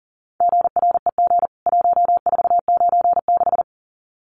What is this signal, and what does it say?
Morse code, heard as a keyed tone, about 40 wpm, 700 Hz. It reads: GREG 1496
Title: CWT Exchange 40wpm